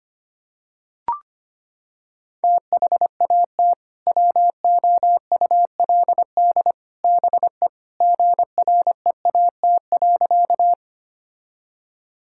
Morse code, heard as a keyed tone, about 25 wpm, 700 Hz. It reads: THAT WOULD BE GREAT.
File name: practice / 25wpm / Sentences with 2-Word Phrases - 25wpm